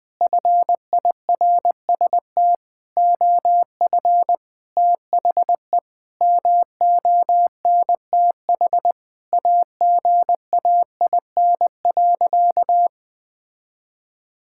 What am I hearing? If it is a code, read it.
Morse code, heard as a keyed tone, about 20 wpm, 700 Hz. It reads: FIRST OF THE MONTH AGAIN.
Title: Sentences from Top 1000 Words - Review Part 5 - 20wpm